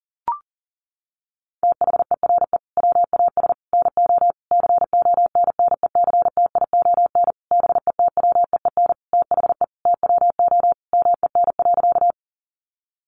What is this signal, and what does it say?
Morse code, heard as a keyed tone, about 40 wpm, 700 Hz. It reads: THERE WAS NO CONNECTION BETWEEN THE TWO MEN.
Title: Sentences from Top 1300 Words 40wpm